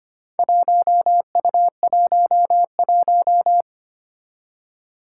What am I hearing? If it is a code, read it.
Morse code, heard as a keyed tone, about 25 wpm, 700 Hz. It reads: 1U11
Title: Four Challenging Letter-Number Combinations 25wpm